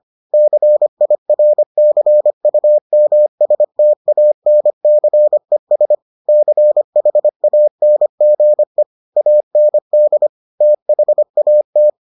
Morse code, heard as a keyed tone, about 25 wpm, 600 Hz. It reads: CIRCUMSTANCES CHANGE AND THAT